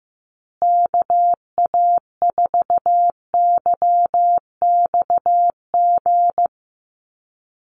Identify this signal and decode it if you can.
Morse code, heard as a keyed tone, about 15 wpm, 700 Hz. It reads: KA4YXG
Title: USA Call Signs Repeated 2x 15wpm